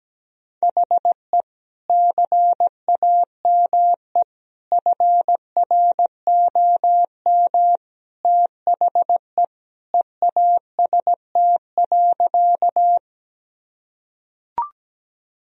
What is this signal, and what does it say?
Morse code, heard as a keyed tone, about 17 wpm, 700 Hz. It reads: HE CAME FROM THE EAST.
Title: Sentences from Top 1000 Words - Review Part 3 - 17wpm